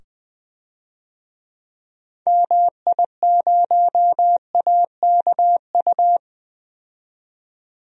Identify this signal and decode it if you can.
Morse code, heard as a keyed tone, about 20 wpm, 700 Hz. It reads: MI0AKU
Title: DXCC-100 Call Signs - 6 Characters - 20wpm